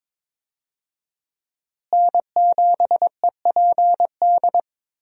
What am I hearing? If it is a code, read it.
Morse code, heard as a keyed tone, about 22 wpm, 700 Hz. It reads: N7EPD
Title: USA Call Signs - 5 Characters - Repeated 2x - 22wpm